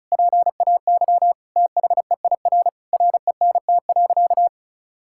Morse code, heard as a keyed tone, about 36 wpm, 700 Hz. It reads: PAY THEIR RENT.
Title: Sentences from Top 1000 Words - Review Part 5 - 35wpm